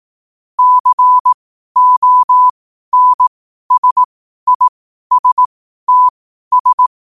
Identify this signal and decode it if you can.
Morse code, heard as a keyed tone, about 18 wpm, spaced as for 12 wpm, 1000 Hz. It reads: CONSISTS
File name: qotdfiles / 12wpm